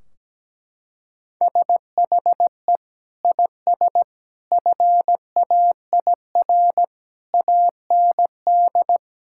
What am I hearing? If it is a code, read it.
Morse code, heard as a keyed tone, about 17 wpm, 700 Hz. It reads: SHE IS FAIR AND
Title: Sentences from Top 1600 Words 17wpm